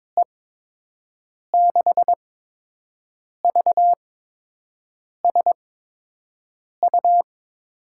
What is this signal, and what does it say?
Morse code, heard as a keyed tone, about 22 wpm, spaced as for 6 wpm, 700 Hz. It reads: E6VSU
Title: USA Call Signs - 6 Characters - 8x Character Spacing - 22wpm